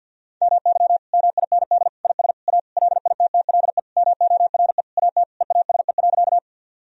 Morse code, heard as a keyed tone, about 52 wpm, 700 Hz. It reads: MY MIND IS A LITTLE MORE AT EASE.
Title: Sentences from Top 1000 Words - Review Part 1 - 50wpm